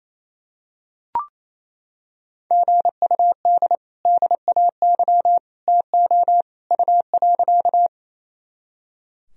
Morse code, heard as a keyed tone, about 28 wpm, 700 Hz. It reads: GUD DAY TO U.
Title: Sentences from Top 200 Words and Abbreviations 28wpm